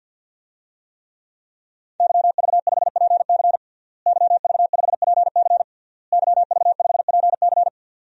Morse code, heard as a keyed tone, about 52 wpm, 700 Hz. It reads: YV5PC YV5PC YV5PC
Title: DXCC-100 Call Signs - 5 Characters - Repeated 3x - 50wpm